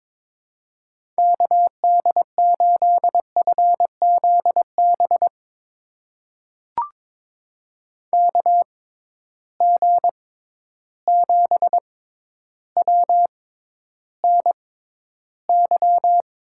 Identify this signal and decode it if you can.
Morse code, heard as a keyed tone, about 22 wpm, spaced as for 11 wpm, 700 Hz. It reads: KD8FZB KG7WNY
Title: USA Call Signs - 6x Character Spacing - 22wpm